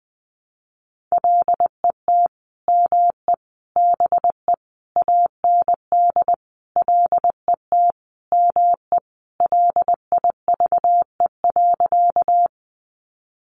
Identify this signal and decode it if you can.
Morse code, heard as a keyed tone, about 20 wpm, 700 Hz. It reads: LET ME BE AND LET ME LIVE.
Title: Sentences from Top 300 Words 20wpm